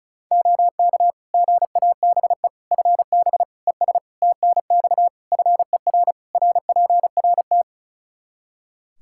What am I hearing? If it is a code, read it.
Morse code, heard as a keyed tone, about 36 wpm, 700 Hz. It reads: OK GABE FB ES TNX FER RPRT
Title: Practice QSO - Exchange 2 - Info - 35wpm